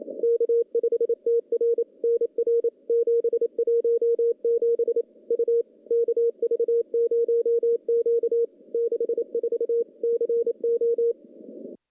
Morse code, heard as a keyed tone, about 28 wpm, 470 Hz. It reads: K5TR NR 717 U KV0Q 64 CO
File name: contests / CW-SS-k5tr- -1300